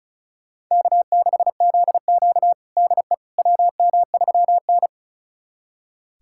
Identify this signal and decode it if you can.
Morse code, heard as a keyed tone, about 36 wpm, 700 Hz. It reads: K6ZQ DE WM3N
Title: USA Callsign DE Callsign - 4 Characters - 35wpm